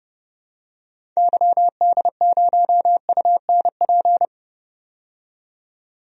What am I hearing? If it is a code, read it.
Morse code, heard as a keyed tone, about 30 wpm, 700 Hz. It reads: YD0UNP
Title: DXCC-100 Call Signs - 6 Characters - 30wpm